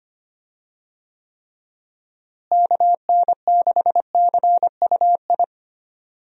Morse code, heard as a keyed tone, about 25 wpm, 700 Hz. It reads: KN6CUI
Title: USA Call Signs - 6 Characters - Repeated 3x - 25wpm